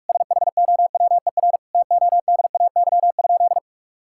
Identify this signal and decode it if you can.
Morse code, heard as a keyed tone, about 42 wpm, 700 Hz. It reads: SHOWER TODAY?
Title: Sentences from Top 1600 Words 45wpm